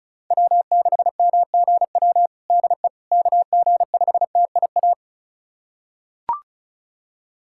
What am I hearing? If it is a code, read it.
Morse code, heard as a keyed tone, about 34 wpm, 700 Hz. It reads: W6MGW DE KG5TIA
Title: USA Callsign DE Callsign 35wpm